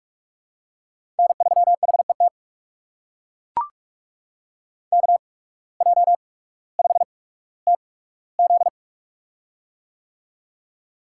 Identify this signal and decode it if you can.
Morse code, heard as a keyed tone, about 43 wpm, spaced as for 19 wpm, 700 Hz. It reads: N3HET KJ5TZ
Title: USA Call Signs - 5 Characters - 8x Character Spacing - 45wpm